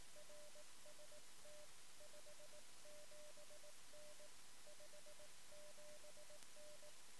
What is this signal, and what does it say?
Morse code, heard as a keyed tone, about 18 wpm, 600 Hz. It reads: RST 57N 57N
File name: sm7lcb / 1 oh2pu-p 13cm